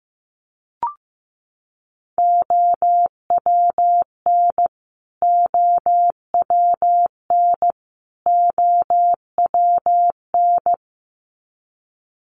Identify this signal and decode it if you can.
Morse code, heard as a keyed tone, about 15 wpm, 700 Hz. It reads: OWN OWN OWN
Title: Top 500 Words Repeated 15wpm